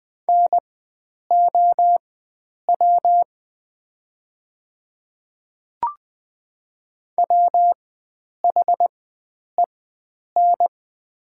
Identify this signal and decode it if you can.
Morse code, heard as a keyed tone, about 20 wpm, spaced as for 9 wpm, 700 Hz. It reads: NOW WHEN
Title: 4x Character Spacing - Top 100 Words - 20wpm